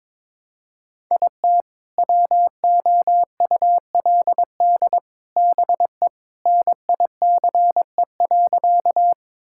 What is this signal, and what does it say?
Morse code, heard as a keyed tone, about 22 wpm, 700 Hz. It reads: IT WOULD BE NICE.
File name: practice / 22wpm / Sentences with 3-Word Phrases - 22wpm